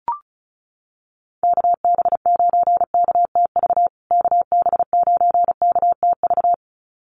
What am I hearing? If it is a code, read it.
Morse code, heard as a keyed tone, about 34 wpm, 700 Hz. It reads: KB9KTV KB9KTV
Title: USA Call Signs Repeated 2x 35wpm